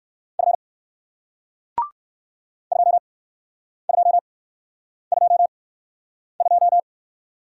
Morse code, heard as a keyed tone, about 57 wpm, spaced as for 9 wpm, 700 Hz. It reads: F 2222